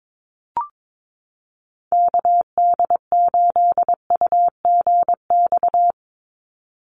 Morse code, heard as a keyed tone, about 22 wpm, 700 Hz. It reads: KD8UGX
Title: USA Call Signs 22wpm